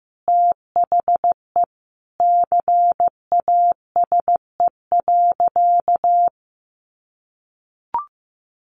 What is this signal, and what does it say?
Morse code, heard as a keyed tone, about 15 wpm, 700 Hz. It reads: THE CASE.
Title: Sentences from Top 1400 Words 15wpm